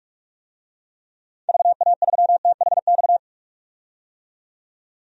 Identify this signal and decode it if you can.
Morse code, heard as a keyed tone, about 42 wpm, 700 Hz. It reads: VA3THX
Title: DXCC-100 Call Signs - 6 Characters - 45wpm